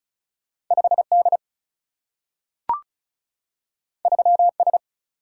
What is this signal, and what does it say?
Morse code, heard as a keyed tone, about 35 wpm, 700 Hz. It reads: 5D 3S
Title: Single-Letter Single-Number 35wpm